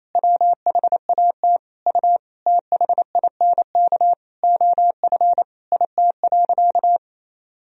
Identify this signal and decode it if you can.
Morse code, heard as a keyed tone, about 28 wpm, 700 Hz. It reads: WHAT U THINK OF IT.
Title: Sentences from Top 200 Words and Abbreviations 28wpm